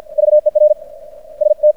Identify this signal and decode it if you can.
Morse code, heard as a keyed tone, about 50 wpm, 600 Hz. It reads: VER ST
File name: archived / CRSnet